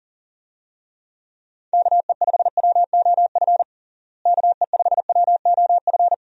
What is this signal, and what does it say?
Morse code, heard as a keyed tone, about 40 wpm, 700 Hz. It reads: KE5WOF KE5WOF
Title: USA Call Signs - 6 Characters - Repeated 3x - 40wpm